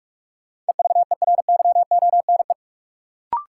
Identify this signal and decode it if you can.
Morse code, heard as a keyed tone, about 43 wpm, 700 Hz. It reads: EVERYONE
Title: Sentences from Top 800 Words - New Word Drill - 45wpm